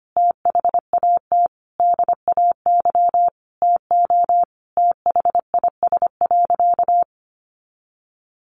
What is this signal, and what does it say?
Morse code, heard as a keyed tone, about 25 wpm, 700 Hz. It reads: THAT DAY TO THIS.
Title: Sentences from Top 1400 Words 25wpm